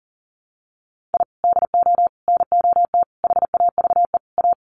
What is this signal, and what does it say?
Morse code, heard as a keyed tone, about 39 wpm, 700 Hz. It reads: I DO NOT HAVE A
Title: Sets of 5 Words 40wpm